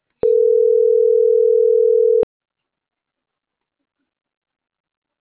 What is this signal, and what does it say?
Ringback tone (North America), 440 + 480 Hz.